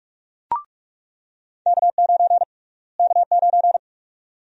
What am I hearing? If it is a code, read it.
Morse code, heard as a keyed tone, about 45 wpm, 700 Hz. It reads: K9 K9
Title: USA Call Sign Prefixes - Repeated 2x - 45wpm